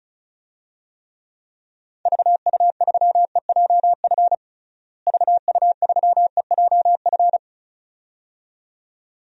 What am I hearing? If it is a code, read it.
Morse code, heard as a keyed tone, about 36 wpm, 700 Hz. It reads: VU3EJF VU3EJF
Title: DXCC-100 Call Signs - 6 Characters - Repeated 2x - 35wpm